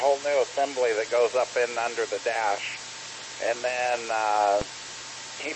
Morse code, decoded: none

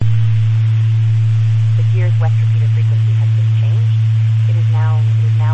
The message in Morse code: none

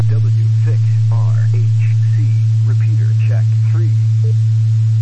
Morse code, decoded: none

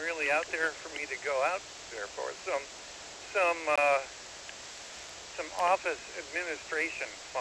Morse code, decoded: none

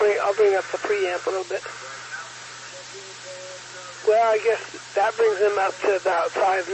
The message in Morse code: none

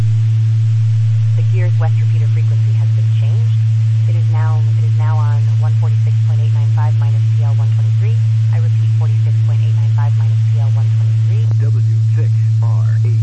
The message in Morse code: none

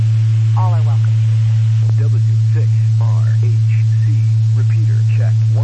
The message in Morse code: none

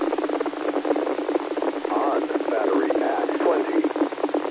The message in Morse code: none

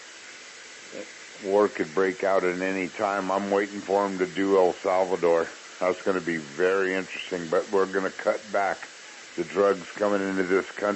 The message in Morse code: none

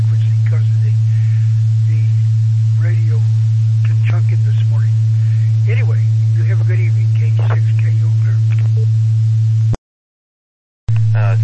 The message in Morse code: none

